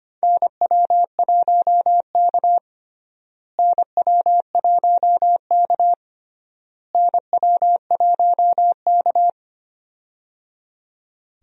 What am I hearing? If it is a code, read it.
Morse code, heard as a keyed tone, about 25 wpm, 700 Hz. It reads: NW1K NW1K NW1K
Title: USA Call Signs - 4 Characters - Repeated 3x - 25wpm